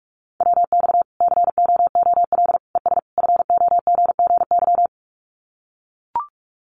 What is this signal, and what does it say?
Morse code, heard as a keyed tone, about 44 wpm, 700 Hz. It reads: WX COOL ES FOGGY
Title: Common QSO Phrases 45wpm